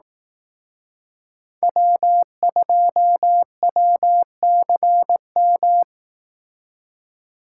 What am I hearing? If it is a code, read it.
Morse code, heard as a keyed tone, about 18 wpm, 700 Hz. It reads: W2WCM
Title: Speed Racing - USA 1x3 Call Signs 15wpm